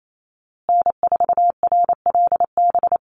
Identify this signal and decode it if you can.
Morse code, heard as a keyed tone, about 28 wpm, 700 Hz. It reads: N4RLB
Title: USA Call Signs Repeated 2x 28wpm